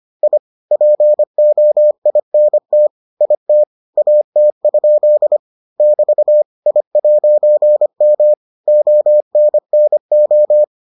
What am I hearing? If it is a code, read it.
Morse code, heard as a keyed tone, about 25 wpm, 600 Hz. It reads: I POINT IT AT? <BT> I'M ONNO